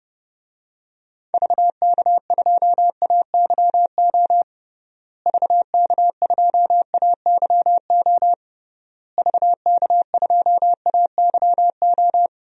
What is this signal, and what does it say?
Morse code, heard as a keyed tone, about 30 wpm, 700 Hz. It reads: VK2AYO VK2AYO VK2AYO
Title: DXCC-100 Call Signs - 6 Characters - Repeated 3x - 30wpm